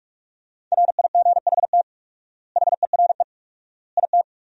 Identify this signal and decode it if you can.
Morse code, heard as a keyed tone, about 47 wpm, 700 Hz. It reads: RIGHT HERE IT
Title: 4x Word-Spacing - Sentences from Top 300 Words - 45wpm